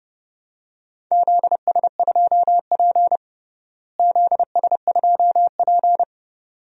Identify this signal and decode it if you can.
Morse code, heard as a keyed tone, about 32 wpm, 700 Hz. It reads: ZS2P ZS2P
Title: DXCC-100 Call Signs - 4 Characters - Repeated 2x - 30wpm